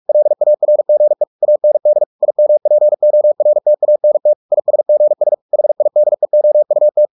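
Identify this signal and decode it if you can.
Morse code, heard as a keyed tone, about 44 wpm, 600 Hz. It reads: LARGE AND IMPORTANT ISGS HIDEOUT
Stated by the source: Morse practice at 45 wpm, the novel Never by Ken Follett in English, from CW med Gnister.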